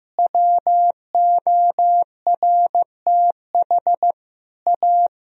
Morse code, heard as a keyed tone, about 15 wpm, 700 Hz. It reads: WORTH A
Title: Sentences from Top 1000 Words - Review Part 4 - 15wpm